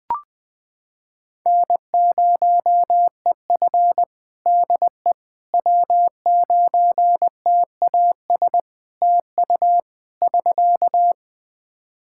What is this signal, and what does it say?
Morse code, heard as a keyed tone, about 20 wpm, 700 Hz. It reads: N0EF DE W9TAS TU <SK>
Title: Practice QSO - Exchange 4 - Ending - 20wpm